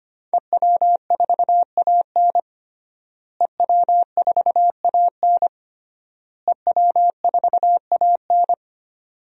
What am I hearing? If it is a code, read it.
Morse code, heard as a keyed tone, about 25 wpm, 700 Hz. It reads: EW4AN EW4AN EW4AN
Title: DXCC-100 Call Signs - 5 Characters - Repeated 3x - 25wpm